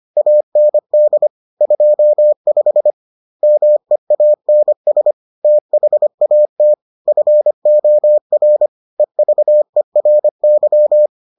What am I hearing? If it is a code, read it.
Morse code, heard as a keyed tone, about 25 wpm, 600 Hz. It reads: AND 25 MEANS THAT FOR EVERY